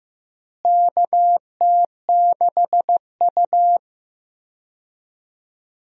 Morse code, heard as a keyed tone, about 15 wpm, 700 Hz. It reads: KT6U
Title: USA Call Signs - 4 Characters - Repeated 2x - 15wpm